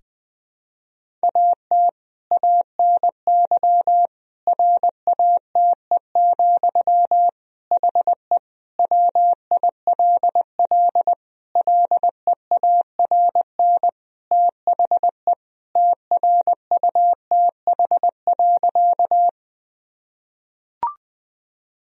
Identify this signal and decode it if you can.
Morse code, heard as a keyed tone, about 20 wpm, 700 Hz. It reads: AT ANY RATE, HE WILL LEARN THE TRUTH.
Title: Sentences from Top 1000 Words - Review Part 4 - 20wpm